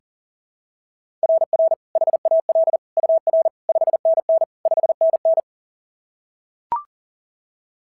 Morse code, heard as a keyed tone, about 41 wpm, 690 Hz. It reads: RR HAL UR 5NN 5NN
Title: SOTA QSO Practice 40wpm